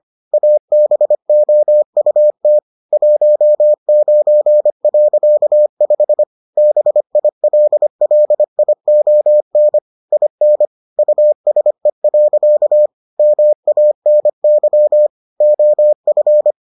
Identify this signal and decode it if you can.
Morse code, heard as a keyed tone, about 25 wpm, 600 Hz. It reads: ABOUT 19.5 BILLION IN USE. MANY OF